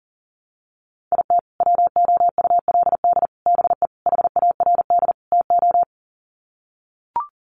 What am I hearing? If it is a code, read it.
Morse code, heard as a keyed tone, about 40 wpm, 700 Hz. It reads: IT WOULD BE HARD TO
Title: Sets of 5 Words Encore2 40wpm